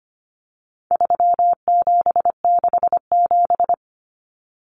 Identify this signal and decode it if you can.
Morse code, heard as a keyed tone, about 25 wpm, 700 Hz. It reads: 3767
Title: Numbers - 4-Digit 25wpm